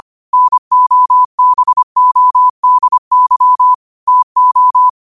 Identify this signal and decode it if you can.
Morse code, heard as a keyed tone, about 25 wpm, 1000 Hz. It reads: NOBODY TO
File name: qotdfiles / 25wpm